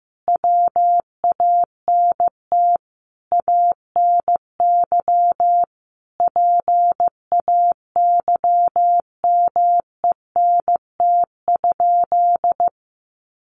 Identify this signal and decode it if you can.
Morse code, heard as a keyed tone, about 15 wpm, 700 Hz. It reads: WANT ANY PAYMENT?
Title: Sentences from Top 1200 Words 15wpm